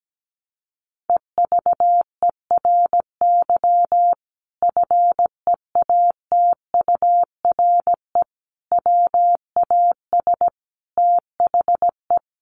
Morse code, heard as a keyed tone, about 17 wpm, 700 Hz. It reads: EVERY FEATURE WAS THE
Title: Sentences from Top 800 Words 17wpm